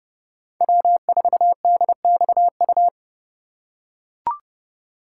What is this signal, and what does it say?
Morse code, heard as a keyed tone, about 30 wpm, 700 Hz. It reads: W4DXU
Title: USA Call Signs - 5 Characters - 4x Character Spacing - 30wpm